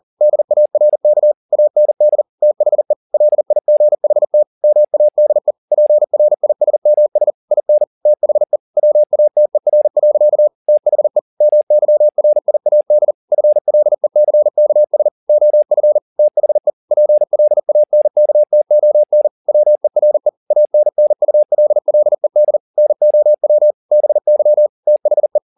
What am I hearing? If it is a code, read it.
Morse code, heard as a keyed tone, about 40 wpm, 600 Hz. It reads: DARK AND THE LIGHT MADE PRISMS IN THE WATER. THE MYRIAD FLECKS OF THE PLANKTON WERE ANNULLED NOW BY THE